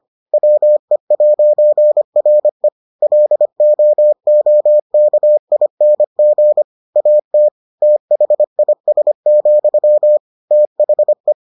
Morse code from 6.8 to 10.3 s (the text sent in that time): AT THIS,